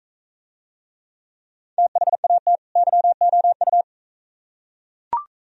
Morse code, heard as a keyed tone, about 42 wpm, 700 Hz. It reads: THAT YOU